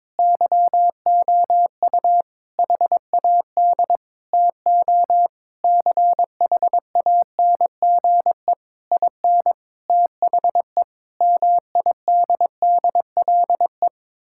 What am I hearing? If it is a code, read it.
Morse code, heard as a keyed tone, about 22 wpm, 690 Hz. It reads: YOU HAD TO CHANGE IN THE MIDDLE
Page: morse-code-ninja-practice-22wpm